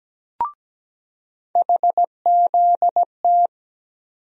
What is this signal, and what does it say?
Morse code, heard as a keyed tone, about 17 wpm, 700 Hz. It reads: HZT
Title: USA Call Sign Postfixes - 3 Letters 17wpm